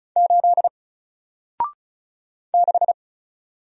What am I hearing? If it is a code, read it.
Morse code, heard as a keyed tone, about 36 wpm, 700 Hz. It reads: 8 6
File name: Single-Letter-Number-35wpm